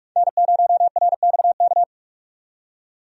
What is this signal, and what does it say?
Morse code, heard as a keyed tone, about 43 wpm, 700 Hz. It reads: N0RXK
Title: USA Call Signs - 5 Characters - 45wpm